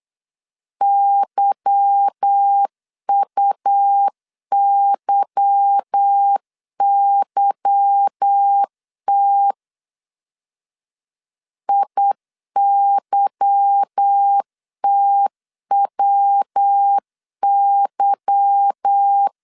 Morse code, heard as a keyed tone, about 8 wpm, 800 Hz. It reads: YUYYT IYTWY
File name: LEZIONI / 8 YJ45 / 8 Y 8wpm 3245